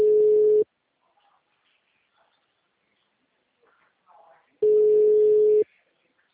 Ringback tone (Europe, 425 Hz).